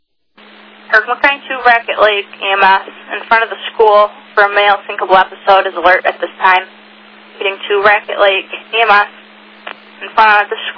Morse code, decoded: none

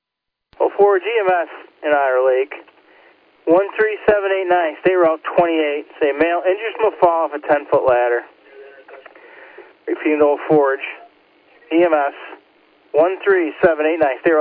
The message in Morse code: none

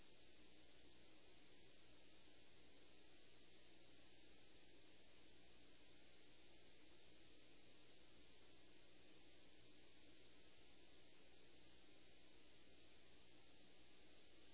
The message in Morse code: none

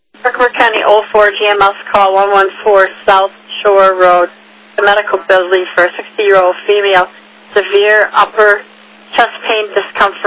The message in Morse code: none

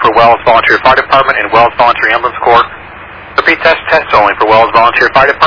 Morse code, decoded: none

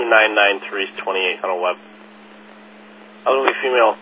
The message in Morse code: none

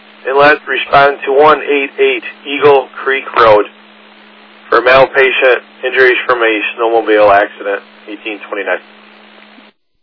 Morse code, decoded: none

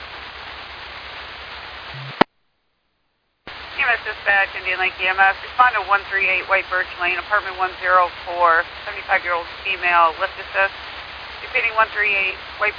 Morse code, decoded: none

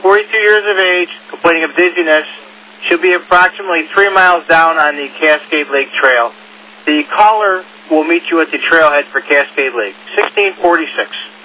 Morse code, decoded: none

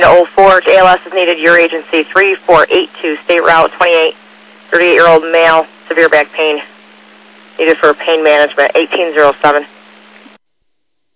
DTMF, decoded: none

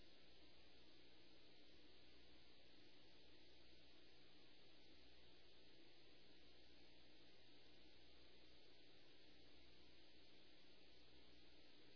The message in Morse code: none

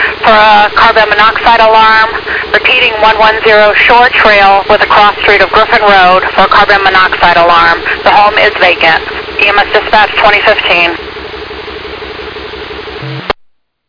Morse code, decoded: none

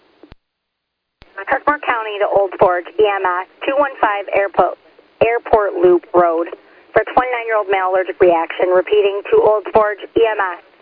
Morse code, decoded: none

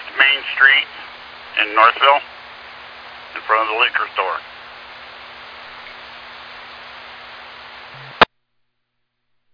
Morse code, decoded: none

